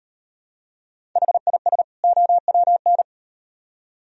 Morse code, heard as a keyed tone, about 38 wpm, 700 Hz. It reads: HIS OWN